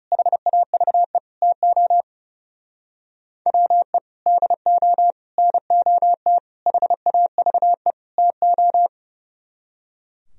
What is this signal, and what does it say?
Morse code, heard as a keyed tone, about 39 wpm, 700 Hz. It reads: HAVE TO WE DO NOT HAVE TO